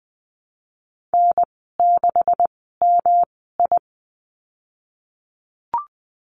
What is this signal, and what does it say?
Morse code, heard as a keyed tone, about 20 wpm, spaced as for 14 wpm, 700 Hz. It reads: N6MI
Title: 245 Lesson 39g - Letter z - Call Signs 20wpm